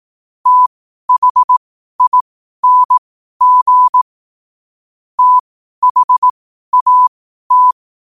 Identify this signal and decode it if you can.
Morse code, heard as a keyed tone, about 18 wpm, spaced as for 12 wpm, 1000 Hz. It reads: THING THAT
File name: 2026-01-07-12wpm